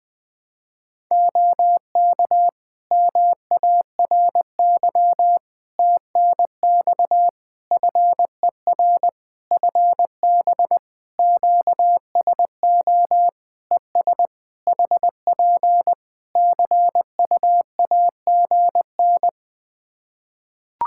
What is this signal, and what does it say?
Morse code, heard as a keyed tone, about 20 wpm, 700 Hz. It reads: OK MARY TNX FER FB QSO ES HP CUAGN
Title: Practice QSO - Exchange 4 - Ending - 20wpm